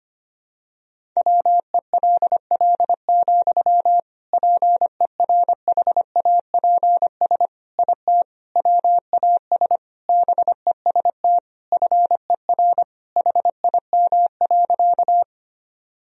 Morse code, heard as a keyed tone, about 25 wpm, 700 Hz. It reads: WELL, PERHAPS IT WAS BEST FER HIM.
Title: Sentences from Top 500 Words and Abbreviations 25wpm